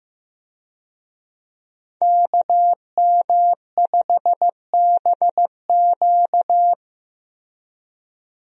Morse code, heard as a keyed tone, about 15 wpm, 700 Hz. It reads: KM5BQ
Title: USA Call Signs - 5 Characters - 15wpm